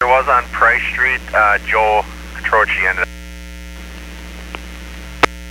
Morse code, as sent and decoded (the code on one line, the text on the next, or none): none